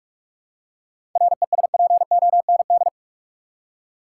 Morse code, heard as a keyed tone, about 46 wpm, 700 Hz. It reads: RESPOND
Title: Seven Letter Words - 45wpm